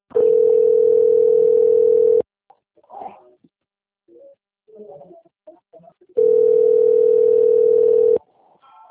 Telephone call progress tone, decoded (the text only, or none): ringback tone